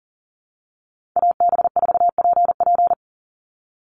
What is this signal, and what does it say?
Morse code, heard as a keyed tone, about 40 wpm, 700 Hz. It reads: AB4PP
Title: NA SOTA Chaser Callsigns 40wpm